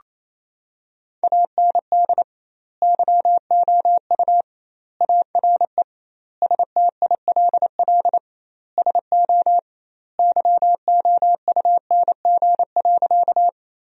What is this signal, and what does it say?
Morse code, heard as a keyed tone, about 28 wpm, 700 Hz. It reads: AND YOU ARE STILL SO YOUNG.
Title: 2x Word-Spacing - Sentences from Top 400 Words - 28wpm